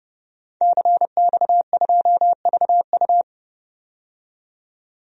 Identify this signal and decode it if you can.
Morse code, heard as a keyed tone, about 31 wpm, 700 Hz. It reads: CX2VU
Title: DXCC-100 Call Signs - 5 Characters - 30wpm